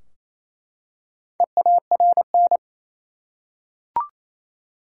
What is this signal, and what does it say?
Morse code, heard as a keyed tone, about 28 wpm, 700 Hz. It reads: EARN
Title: Four Letter Words - 28wpm